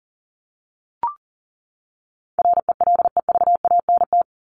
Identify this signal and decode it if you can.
Morse code, heard as a keyed tone, about 40 wpm, 700 Hz. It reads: RELEVANT
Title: Top 1001-1250 Words 40wpm